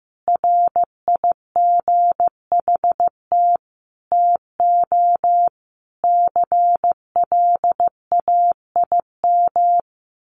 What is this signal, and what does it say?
Morse code, heard as a keyed tone, about 15 wpm, 700 Hz. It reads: RIGHT TO CLAIM
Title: Sentences from Top 1500 Words 15wpm